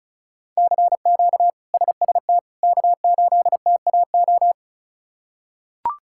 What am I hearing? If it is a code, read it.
Morse code, heard as a keyed tone, about 34 wpm, 700 Hz. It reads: CQ SST K8TAO
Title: K1USN SST QSO Practice - 35wpm